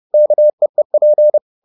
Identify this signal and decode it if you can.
Morse code, heard as a keyed tone, about 30 wpm, 600 Hz. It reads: KEEP